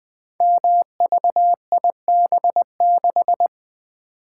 Morse code, heard as a keyed tone, about 20 wpm, 700 Hz. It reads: MVIB6
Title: Code Groups - 5 Letter:Numbers - 20wpm